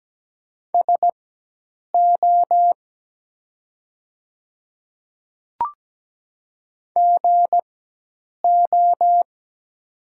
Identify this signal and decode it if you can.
Morse code, heard as a keyed tone, about 17 wpm, spaced as for 8 wpm, 700 Hz. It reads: SO GO
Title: 4x Character Spacing - Top 100 Words - 17wpm